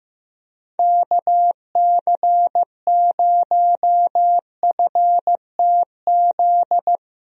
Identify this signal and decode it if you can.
Morse code, heard as a keyed tone, about 15 wpm, 700 Hz. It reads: KC0FTZ
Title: USA Call Signs - 6 Characters - 15wpm